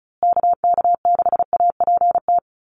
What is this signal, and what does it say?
Morse code, heard as a keyed tone, about 34 wpm, 700 Hz. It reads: KK6APT
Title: USA Call Signs Repeated 2x 35wpm